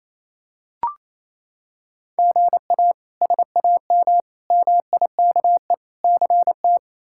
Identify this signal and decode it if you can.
Morse code, heard as a keyed tone, about 28 wpm, 700 Hz. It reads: GA SAM MIKE CT
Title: K1USN SST QSO Practice - 28wpm